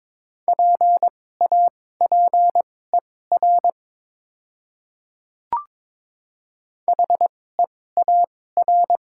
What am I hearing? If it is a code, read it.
Morse code, heard as a keyed tone, about 22 wpm, spaced as for 16 wpm, 700 Hz. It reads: PAPER HEAR
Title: 2x Character Spacing - Top 300 Words - 22wpm